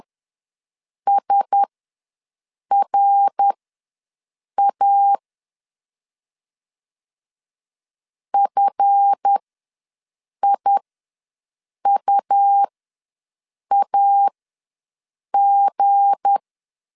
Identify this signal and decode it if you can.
Morse code, heard as a keyed tone, about 10 wpm, spaced as for 6 wpm, 800 Hz. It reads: SRA FIUAG
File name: LEZIONI / 7 GF9Z / 7 GF ws 10wpm 3918